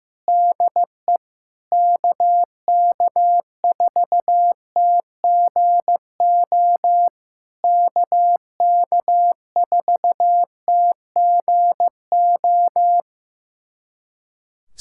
Morse code, heard as a keyed tone, about 15 wpm, 700 Hz. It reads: DE KK4TGO KK4TGO
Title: SOTA QSO Practice 15wpm